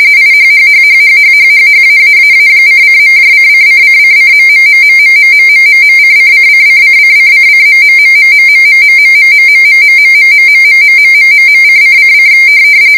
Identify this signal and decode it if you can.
RTTY, about 45.45 baud. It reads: IHHHHHHHHHAAAA::.::::::..........::::::::::........................::::